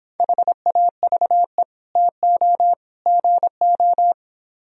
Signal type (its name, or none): Morse code